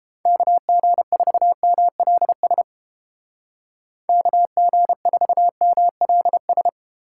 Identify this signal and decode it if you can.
Morse code, heard as a keyed tone, about 33 wpm, 700 Hz. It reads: KG4MLS KG4MLS